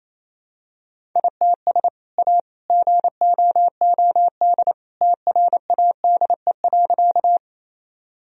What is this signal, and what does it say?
Morse code, heard as a keyed tone, about 28 wpm, 700 Hz. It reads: ITS A GOOD TRADE.
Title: Sentences from Top 1000 Words - Review Part 4 - 28wpm